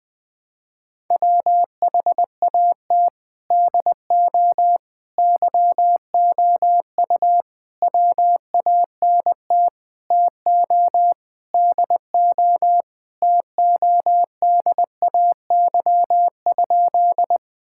Morse code, heard as a keyed tone, about 20 wpm, 700 Hz. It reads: WHAT DO YOU WANT TO DO TODAY?
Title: Sentences with 4-Word Phrases - 20wpm